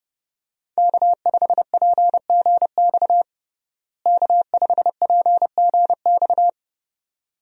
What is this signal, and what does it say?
Morse code, heard as a keyed tone, about 30 wpm, 700 Hz. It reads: K5PGX K5PGX
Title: USA Call Signs - 5 Characters - Repeated 2x - 30wpm